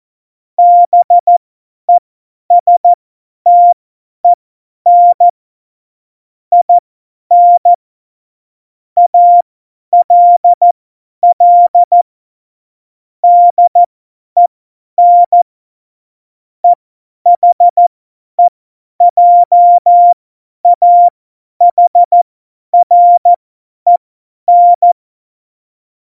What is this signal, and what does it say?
Morse code, heard as a keyed tone, about 14 wpm, spaced as for 10 wpm, 700 Hz. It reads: BESTEN IN ALL DEN EHEJAHREN